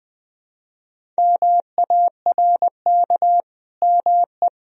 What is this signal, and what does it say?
Morse code, heard as a keyed tone, about 20 wpm, 700 Hz. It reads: MARK ME
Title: Sentences from Top 1000 Words - Review Part 1 - 20wpm